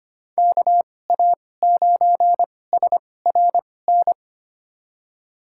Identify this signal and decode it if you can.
Morse code, heard as a keyed tone, about 25 wpm, spaced as for 18 wpm, 700 Hz. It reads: KA9SRN
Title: USA Call Signs - 6 Characters - 2x Character Spacing - 25wpm